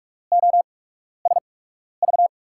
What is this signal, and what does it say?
Morse code, heard as a keyed tone, about 42 wpm, spaced as for 12 wpm, 700 Hz. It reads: OSV
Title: USA Call Signs - 5 Characters - 8x Character Spacing - 45wpm